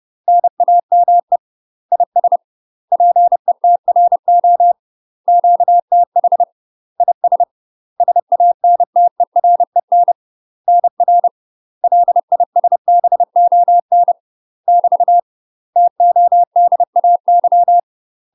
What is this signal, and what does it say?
Morse code, heard as a keyed tone, about 30 wpm, 700 Hz. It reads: NAME IS PETRO QTH IS SANTEREN NR LISBON <BT> TODAY
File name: UK MASTER QSOs TRAINING / 30WPM / UK MASTER QSOs 0049